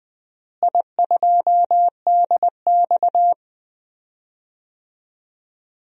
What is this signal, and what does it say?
Morse code, heard as a keyed tone, about 21 wpm, 700 Hz. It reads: I2DX
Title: DXCC-100 Call Signs - 4 Characters - Repeated 2x - 20wpm